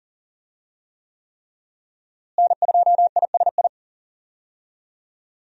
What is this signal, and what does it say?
Morse code, heard as a keyed tone, about 40 wpm, 700 Hz. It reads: N2ISI
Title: USA Call Signs - 5 Characters - Repeated 3x - 40wpm